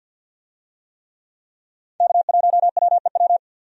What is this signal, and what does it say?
Morse code, heard as a keyed tone, about 46 wpm, 700 Hz. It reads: K1WEW
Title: USA Call Signs - 5 Characters - 50wpm